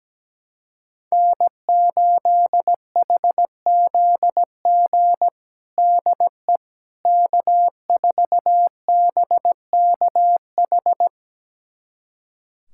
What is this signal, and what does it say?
Morse code, heard as a keyed tone, about 17 wpm, 700 Hz. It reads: N8HZG DE K4BKH
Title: USA Callsign DE Callsign 17wpm